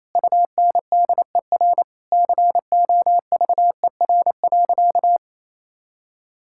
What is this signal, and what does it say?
Morse code, heard as a keyed tone, about 28 wpm, 700 Hz. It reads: UNDER COVER.
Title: Sentences from Top 1000 Words - Review Part 6 - 28wpm